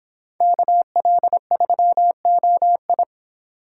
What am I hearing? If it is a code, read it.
Morse code, heard as a keyed tone, about 30 wpm, 700 Hz. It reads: KL3OI